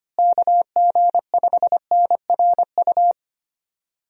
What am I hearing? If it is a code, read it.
Morse code, heard as a keyed tone, about 25 wpm, 700 Hz. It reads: KG5NRU
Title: USA Call Signs - 6 Characters - Repeated 3x - 25wpm